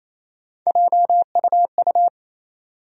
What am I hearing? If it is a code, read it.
Morse code, heard as a keyed tone, about 28 wpm, 700 Hz. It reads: JUU
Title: USA Call Sign Postfixes - 3 Letters 28wpm